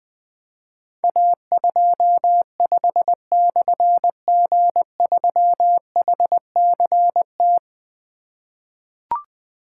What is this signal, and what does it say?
Morse code, heard as a keyed tone, about 20 wpm, 700 Hz. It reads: A25/G3HCT
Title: Difficult Call Signs 20wpm